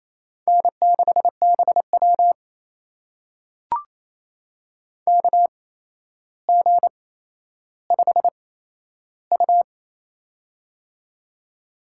Morse code, heard as a keyed tone, about 28 wpm, spaced as for 12 wpm, 700 Hz. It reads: N6BW KG5U
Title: USA Call Signs - 4 Characters - 8x Character Spacing - 28wpm